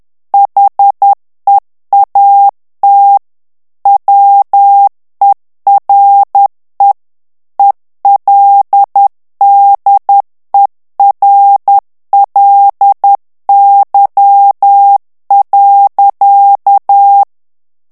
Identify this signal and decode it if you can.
Morse code, heard as a keyed tone, about 10 wpm, 800 Hz. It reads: HEAT WERE ELDERLY.
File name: morse mp3 / ANGLAIS texte clair 2 vitesse 10 mots minute